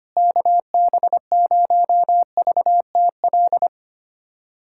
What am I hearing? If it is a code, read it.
Morse code, heard as a keyed tone, about 25 wpm, 700 Hz. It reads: KB0VTL
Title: USA Callsign DE Callsign 25wpm